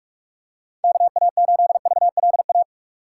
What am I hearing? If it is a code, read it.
Morse code, heard as a keyed tone, about 43 wpm, 700 Hz. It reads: KA8VLA
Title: USA Call Signs - 2x Character Spacing - 45wpm